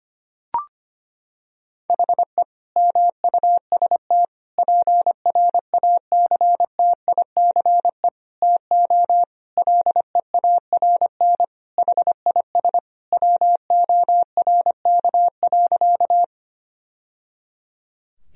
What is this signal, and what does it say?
Morse code, heard as a keyed tone, about 25 wpm, 700 Hz. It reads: HE MUST PRACTICE TO LEARN HIS WORK.
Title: Sentences from Top 1000 Words - Review Part 4 - 25wpm